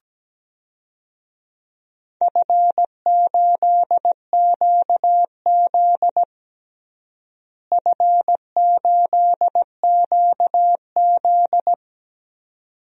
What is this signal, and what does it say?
Morse code, heard as a keyed tone, about 17 wpm, 700 Hz. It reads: F8QZ F8QZ
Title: DXCC-100 Call Signs - 4 Characters - Repeated 2x - 17wpm